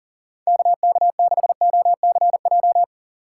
Morse code, heard as a keyed tone, about 49 wpm, 700 Hz. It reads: KK6OCJ
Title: Speed Racing - USA Call Signs 40wpm